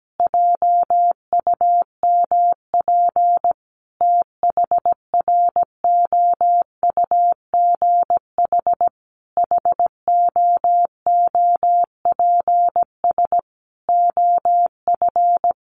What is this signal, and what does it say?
Morse code, heard as a keyed tone, about 17 wpm, 700 Hz. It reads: JUMP THROUGH HOOPS OF
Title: Easy Phrases 17wpm